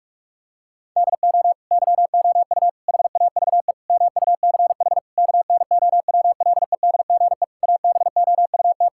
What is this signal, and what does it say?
Morse code, heard as a keyed tone, about 41 wpm, 700 Hz. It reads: DO YOU HAVE MUCH KNOWLEDGE ABOUT
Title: Sentences from Top 1000 Words - Review Part 6 - 45wpm